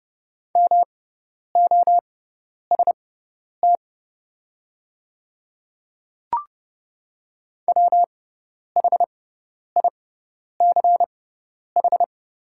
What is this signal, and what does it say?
Morse code, heard as a keyed tone, about 30 wpm, spaced as for 10 wpm, 700 Hz. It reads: MOST WHICH
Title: 6x Character Spacing - Top 300 Words - 30wpm